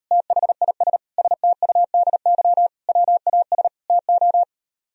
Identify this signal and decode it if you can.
Morse code, heard as a keyed tone, about 38 wpm, 700 Hz. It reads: THIS STUDY WAS TO